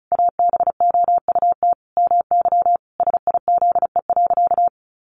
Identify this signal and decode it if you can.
Morse code, heard as a keyed tone, about 34 wpm, 700 Hz. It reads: ABOUT MY SIZE.
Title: Sentences from Top 400 Words 35wpm